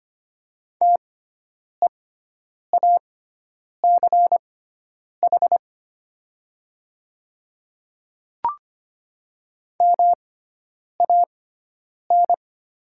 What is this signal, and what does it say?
Morse code, heard as a keyed tone, about 25 wpm, spaced as for 9 wpm, 700 Hz. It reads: TEACH MAN
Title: 6x Character Spacing - Top 500 Words - 25wpm